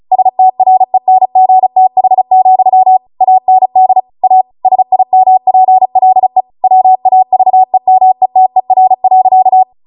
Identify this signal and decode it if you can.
Morse code, heard as a keyed tone, about 36 wpm, 750 Hz. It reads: STRENGTH, AND A SIMPLE WAVEMETER.